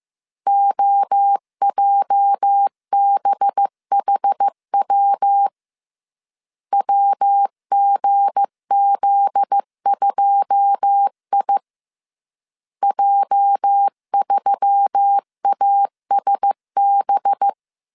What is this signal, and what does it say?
Morse code, heard as a keyed tone, about 15 wpm, 800 Hz. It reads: OJBHW WGZ2I J3ASB
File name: CW / DET / 1 9 1500C S C 14wpm 3245